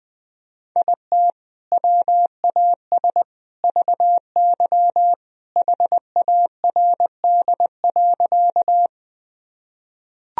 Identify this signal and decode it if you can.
Morse code, heard as a keyed tone, about 20 wpm, 700 Hz. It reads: IT WAS VY HARD.
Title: Sentences from Top 300 Words and Abbreviations 20wpm